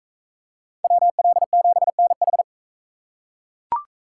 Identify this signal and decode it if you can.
Morse code, heard as a keyed tone, about 40 wpm, 700 Hz. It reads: WL7NH